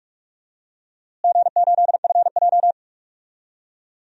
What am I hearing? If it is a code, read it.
Morse code, heard as a keyed tone, about 44 wpm, 700 Hz. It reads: G8FJ
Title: DXCC-100 Call Signs - 4 Characters - Repeated 3x - 45wpm